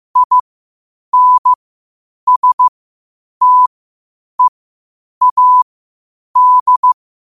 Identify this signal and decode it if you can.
Morse code, heard as a keyed tone, about 15 wpm, spaced as for 8 wpm, 1000 Hz. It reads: INSTEAD